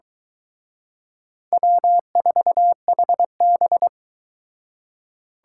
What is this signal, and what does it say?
Morse code, heard as a keyed tone, about 23 wpm, 700 Hz. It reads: W4HB